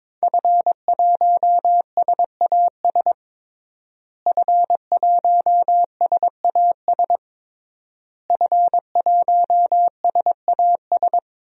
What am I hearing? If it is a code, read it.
Morse code, heard as a keyed tone, about 22 wpm, 700 Hz. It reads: F1SAS F1SAS F1SAS
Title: DXCC-100 Call Signs - 5 Characters - Repeated 3x - 22wpm